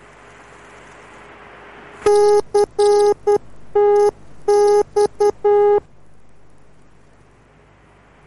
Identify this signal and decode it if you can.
Morse code, heard as a keyed tone, about 10 wpm, 410 Hz. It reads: CTX